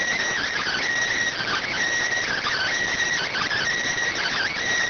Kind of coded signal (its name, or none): Emergency Alert System (SAME) tones